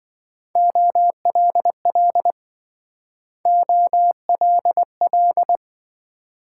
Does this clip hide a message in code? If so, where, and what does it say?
Yes: Morse code from 0.6 to 5.6 s, heard as a keyed tone, about 27 wpm, 700 Hz. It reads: OLL OLL